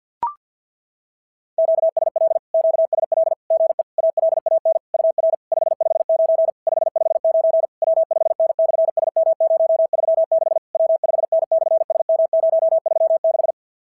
Morse code, heard as a keyed tone, about 50 wpm, 650 Hz. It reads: QSL QSL GE ALAN UR 559 559 W5N/SM036 W5N/SM036
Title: SOTA QSO Practice 50wpm